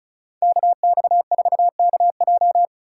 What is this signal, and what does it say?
Morse code, heard as a keyed tone, about 36 wpm, 700 Hz. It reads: KX4KJ
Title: USA Call Signs - 5 Characters - 2x Character Spacing - 35wpm